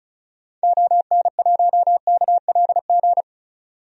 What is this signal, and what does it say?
Morse code, heard as a keyed tone, about 34 wpm, 700 Hz. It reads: ON1KLG
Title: DXCC-100 Call Signs - 6 Characters - Repeated 3x - 35wpm